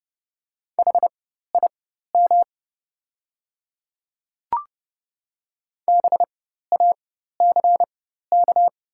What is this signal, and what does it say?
Morse code, heard as a keyed tone, about 29 wpm, spaced as for 14 wpm, 700 Hz. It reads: HIM BACK
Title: 4x Character Spacing - Top 300 Words - 30wpm